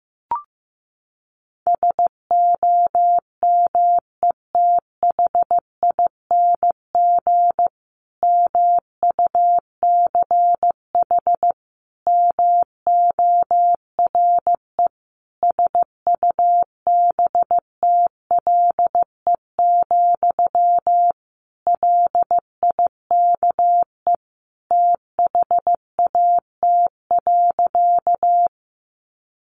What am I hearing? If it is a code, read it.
Morse code, heard as a keyed tone, about 15 wpm, 700 Hz. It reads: SOMETHING MUCH MORE SUBTLE, LIKE THAT.
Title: A Sound Of Thunder 15wpm